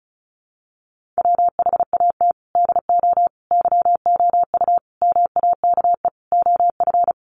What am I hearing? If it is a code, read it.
Morse code, heard as a keyed tone, about 34 wpm, 700 Hz. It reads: WHAT DO YOU MAKE OF
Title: Sets of 5 Words 35wpm